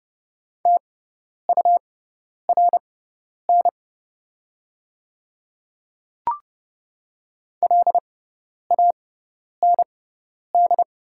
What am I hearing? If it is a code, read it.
Morse code, heard as a keyed tone, about 30 wpm, spaced as for 10 wpm, 700 Hz. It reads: TURN LAND
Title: 6x Character Spacing - Top 200 Words - 30wpm